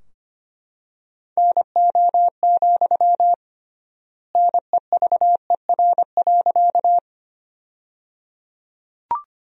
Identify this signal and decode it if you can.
Morse code, heard as a keyed tone, about 25 wpm, 700 Hz. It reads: NO, NEVER.
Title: 3x Word-Spacing - Sentences from Top 300 Words - 25wpm